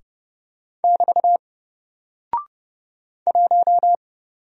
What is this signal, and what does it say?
Morse code, heard as a keyed tone, about 30 wpm, 700 Hz. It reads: <BT> 1
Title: Single Character (Letter:Number:Punctuation:Prosign) 30wpm